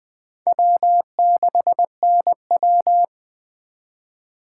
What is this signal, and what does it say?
Morse code, heard as a keyed tone, about 19 wpm, 700 Hz. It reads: W6NW